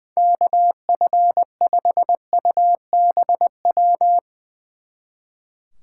Morse code, heard as a keyed tone, about 22 wpm, 700 Hz. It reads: KF5UBW
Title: Speed Racing - USA 2x3 Call Signs 20wpm